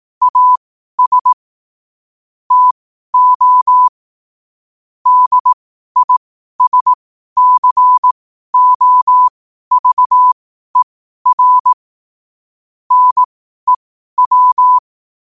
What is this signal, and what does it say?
Morse code, heard as a keyed tone, about 18 wpm, spaced as for 12 wpm, 1000 Hz. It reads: AS TO DISCOVER NEW
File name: qotdfiles / 12wpm